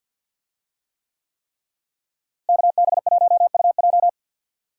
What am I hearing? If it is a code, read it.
Morse code, heard as a keyed tone, about 50 wpm, 700 Hz. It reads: KB1UJ
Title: USA Call Signs - 5 Characters - Repeated 3x - 50wpm